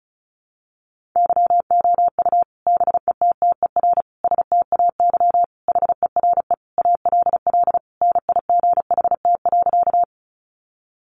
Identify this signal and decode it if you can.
Morse code, heard as a keyed tone, about 34 wpm, 700 Hz. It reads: YOU BETTER STAY HERE ALL NIGHT.
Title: Sentences from Top 500 Words 35wpm